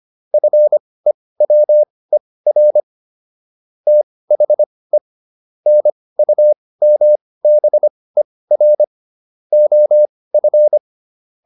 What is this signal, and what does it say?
Morse code, heard as a keyed tone, about 25 wpm, spaced as for 18 wpm, 600 Hz. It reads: FEWER THE NUMBER OF